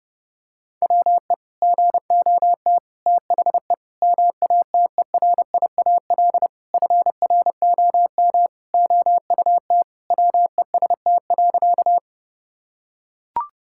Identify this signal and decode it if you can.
Morse code, heard as a keyed tone, about 30 wpm, 700 Hz. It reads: WE GOT THE MATERIAL FROM OUT WEST.
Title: Sentences from Top 1000 Words - Review Part 5 - 30wpm